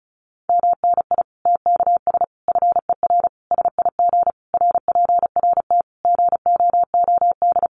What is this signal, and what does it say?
Morse code, heard as a keyed tone, about 34 wpm, 700 Hz. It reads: MNI TKS FER SIG RPRT GOOD
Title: Common QSO Phrases 35wpm